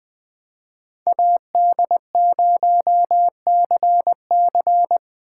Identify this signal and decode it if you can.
Morse code, heard as a keyed tone, about 22 wpm, 700 Hz. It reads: AD0CC